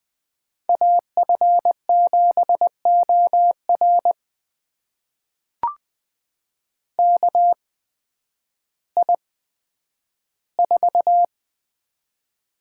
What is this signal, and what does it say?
Morse code, heard as a keyed tone, about 20 wpm, spaced as for 11 wpm, 700 Hz. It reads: AF7OR KI4
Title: USA Call Signs - 8x Character Spacing - 20wpm